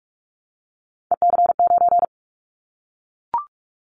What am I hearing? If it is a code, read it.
Morse code, heard as a keyed tone, about 44 wpm, 700 Hz. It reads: EC9
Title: Callsign Prefixes for Top 100 DXCC - 45wpm